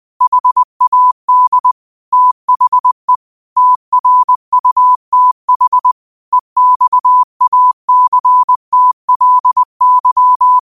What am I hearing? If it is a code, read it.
Morse code, heard as a keyed tone, about 20 wpm, 1000 Hz. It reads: HAD THE TRUTH EXACTLY
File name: qotdfiles / 20wpm